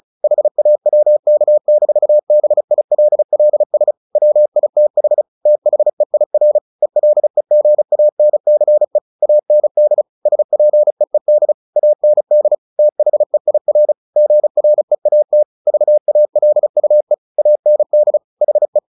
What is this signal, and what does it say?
Morse code, heard as a keyed tone, about 34 wpm, 600 Hz. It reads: HAWK-BILLS WITH THEIR ELEGANCE AND SPEED AND THEIR GREAT VALUE AND HE
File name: the old man and the sea / the old man and the sea 1 35wpm